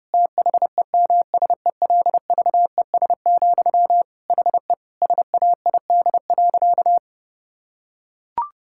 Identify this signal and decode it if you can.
Morse code, heard as a keyed tone, about 30 wpm, 700 Hz. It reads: THEMSELVES, HE SAID.
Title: Sentences from Top 1000 Words - Review Part 6 - 30wpm